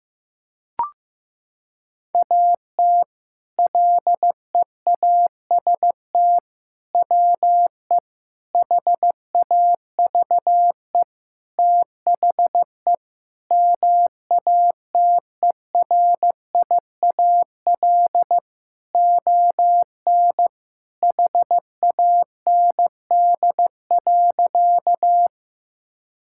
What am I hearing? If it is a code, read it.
Morse code, heard as a keyed tone, about 15 wpm, 700 Hz. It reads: AT LEAST WE HAVE THE MATERIAL ON HAND.
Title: Sentences from Top 1000 Words - Review Part 5 - 15wpm